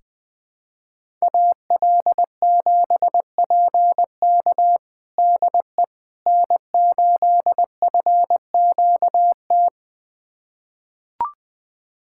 Morse code, heard as a keyed tone, about 20 wpm, 700 Hz. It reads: AL7PK DE N8FQT
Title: USA Callsign DE Callsign - 5 Characters - 20wpm